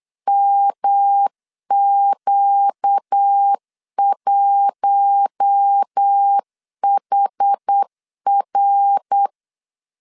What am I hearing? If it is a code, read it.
Morse code, heard as a keyed tone, about 8 wpm, 800 Hz. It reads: MQ1HR
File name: LEZIONI / 8 YJ45 / 8 YJ4 8wpm 3246